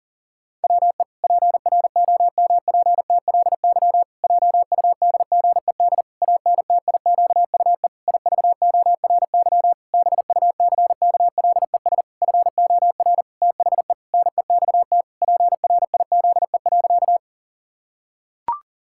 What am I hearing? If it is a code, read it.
Morse code, heard as a keyed tone, about 37 wpm, 700 Hz. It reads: WE PROMPTLY JUDGED ANTIQUE IVORY BUCKLES FOR THE NEXT PRIZE.
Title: Pangram Challenge v2 - 40wpm